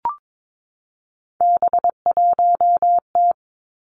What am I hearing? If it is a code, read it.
Morse code, heard as a keyed tone, about 22 wpm, 700 Hz. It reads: B1T
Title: Three Challenging Letter-Number Combinations 22wpm